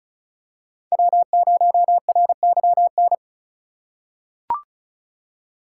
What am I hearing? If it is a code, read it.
Morse code, heard as a keyed tone, about 34 wpm, 700 Hz. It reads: W0RYN
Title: USA Call Signs - 5 Characters - 3x Character Spacing - 35wpm